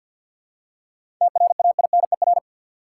Morse code, heard as a keyed tone, about 52 wpm, 700 Hz. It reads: TRAINER
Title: Seven Letter Words - 50wpm